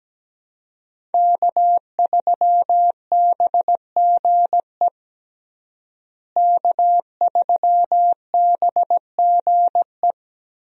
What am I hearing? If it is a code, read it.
Morse code, heard as a keyed tone, about 17 wpm, 700 Hz. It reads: K3BGE K3BGE
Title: USA Call Signs - 5 Characters - Repeated 2x - 17wpm